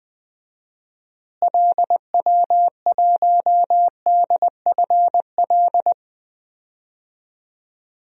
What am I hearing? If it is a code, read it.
Morse code, heard as a keyed tone, about 21 wpm, 700 Hz. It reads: LW1DFL
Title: DXCC-100 Call Signs - 6 Characters - Repeated 3x - 20wpm